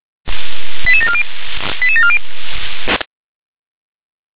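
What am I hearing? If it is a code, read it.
Five-tone selective call (ZVEI), heard as quick runs of five tones, tones of about 70 ms. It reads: 88740 88740
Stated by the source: ZVEI five-tone selective call received on SDR.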